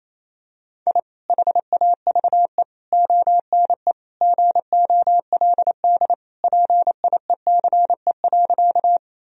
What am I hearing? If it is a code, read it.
Morse code, heard as a keyed tone, about 28 wpm, 700 Hz. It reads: I HAVE ONE GOLD PIECE.
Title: Sentences from Top 1000 Words - Review Part 3 - 28wpm